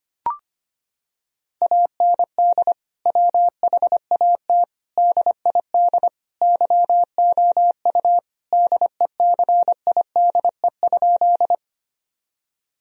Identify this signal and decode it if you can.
Morse code, heard as a keyed tone, about 25 wpm, 700 Hz. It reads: AND WHAT DID YOU DECIDE?
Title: Sentences from Top 1000 Words - Review Part 3 - 25wpm